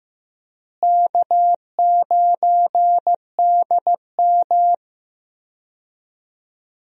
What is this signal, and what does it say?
Morse code, heard as a keyed tone, about 15 wpm, 700 Hz. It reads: K9DM
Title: USA Call Signs - 4 Characters - Repeated 3x - 15wpm